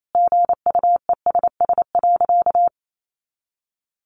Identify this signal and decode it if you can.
Morse code, heard as a keyed tone, about 28 wpm, 700 Hz. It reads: GUESS.
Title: Sentences from Top 1500 Words 28wpm